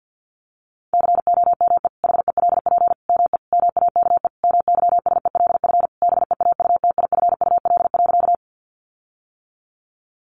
Morse code, heard as a keyed tone, about 48 wpm, 700 Hz. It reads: COME HELP ME MAKE MYSELF BEAUTIFUL.
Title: Sentences from Top 1300 Words 50wpm